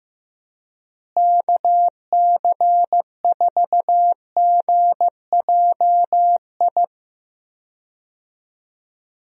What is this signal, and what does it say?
Morse code, heard as a keyed tone, about 15 wpm, 700 Hz. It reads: KC4GJI
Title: USA Call Signs - 6 Characters - Repeated 2x - 15wpm